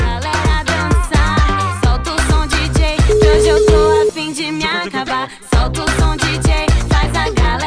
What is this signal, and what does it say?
Ringback tone (Europe, 425 Hz), with noise.